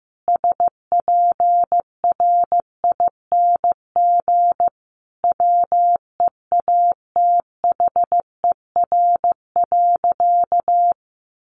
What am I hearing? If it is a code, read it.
Morse code, heard as a keyed tone, about 15 wpm, 700 Hz. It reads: SPRING WEATHER.
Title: Sentences from Top 1200 Words 15wpm